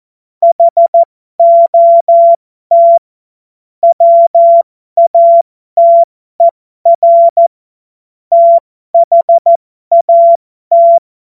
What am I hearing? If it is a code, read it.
Morse code, heard as a keyed tone, about 14 wpm, spaced as for 12 wpm, 680 Hz. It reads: HOT WATER THAT
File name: Morsekurs / Witze